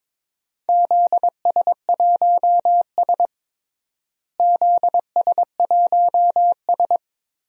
Morse code, heard as a keyed tone, about 22 wpm, 700 Hz. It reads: ZS1S ZS1S
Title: DXCC-100 Call Signs - 4 Characters - Repeated 2x - 22wpm